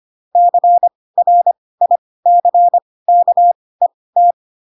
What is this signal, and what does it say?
Morse code, heard as a keyed tone, about 25 wpm, spaced as for 18 wpm, 700 Hz. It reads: CRICKET
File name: Pinocchio / Pinocchio ch4 12 25 ik1hge